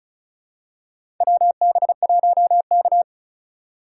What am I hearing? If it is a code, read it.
Morse code, heard as a keyed tone, about 36 wpm, 700 Hz. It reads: WB1K
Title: USA Call Signs - 4 Characters - 2x Character Spacing - 35wpm